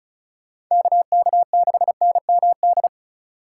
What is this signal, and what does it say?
Morse code, heard as a keyed tone, about 37 wpm, 700 Hz. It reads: KK6NMD
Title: Speed Racing - USA Call Signs 30wpm